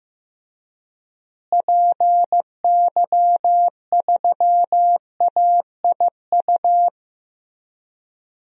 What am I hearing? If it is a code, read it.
Morse code, heard as a keyed tone, about 15 wpm, 700 Hz. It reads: PY3AIU
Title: DXCC-100 Call Signs - 6 Characters - Repeated 3x - 15wpm